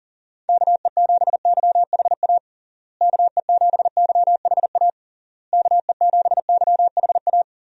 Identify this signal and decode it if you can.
Morse code, heard as a keyed tone, about 40 wpm, 700 Hz. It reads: KE7YHA KE7YHA KE7YHA
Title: USA Call Signs - 6 Characters - Repeated 3x - 40wpm